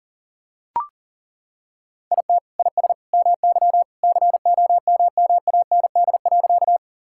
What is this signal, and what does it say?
Morse code, heard as a keyed tone, about 40 wpm, 700 Hz. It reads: IT IS MY COMMAND.
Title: Sentences from Top 1600 Words 40wpm